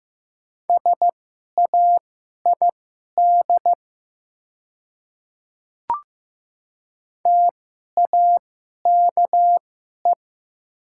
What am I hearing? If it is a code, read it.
Morse code, heard as a keyed tone, about 15 wpm, spaced as for 11 wpm, 690 Hz. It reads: SAID TAKE
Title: 2x Character Spacing - Top 200 Words - 15wpm